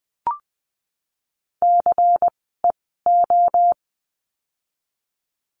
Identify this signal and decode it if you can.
Morse code, heard as a keyed tone, about 20 wpm, spaced as for 15 wpm, 700 Hz. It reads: CEO
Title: 094 Lesson 16f - Letter c - Words 20wpm